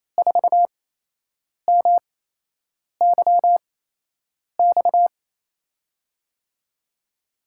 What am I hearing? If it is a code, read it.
Morse code, heard as a keyed tone, about 28 wpm, 700 Hz. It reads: 4 M Y X
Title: USA Call Signs - 6 Characters - 8x Character Spacing - 28wpm